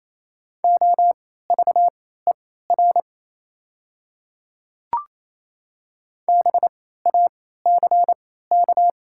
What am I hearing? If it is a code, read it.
Morse code, heard as a keyed tone, about 28 wpm, spaced as for 16 wpm, 700 Hz. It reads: OVER BACK
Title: 3x Character Spacing - Top 200 Words - 28wpm